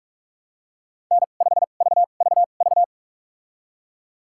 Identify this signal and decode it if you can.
Morse code, heard as a keyed tone, about 46 wpm, spaced as for 33 wpm, 700 Hz. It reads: N5VVV
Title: USA Call Signs - 5 Characters - 2x Character Spacing - 45wpm